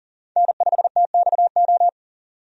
Morse code, heard as a keyed tone, about 40 wpm, 700 Hz. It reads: N5TXO
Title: USA Call Signs - 5 Characters - 40wpm